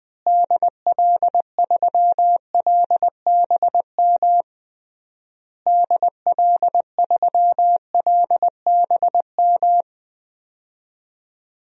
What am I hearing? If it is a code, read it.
Morse code, heard as a keyed tone, about 20 wpm, 700 Hz. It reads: DL3LBM DL3LBM
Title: DXCC-100 Call Signs - 6 Characters - Repeated 3x - 20wpm